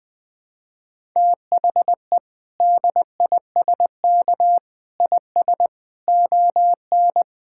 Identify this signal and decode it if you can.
Morse code, heard as a keyed tone, about 21 wpm, 700 Hz. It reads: THE DISK IS ON
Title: Sentences from Top 1000 Words - Review Part 6 - 20wpm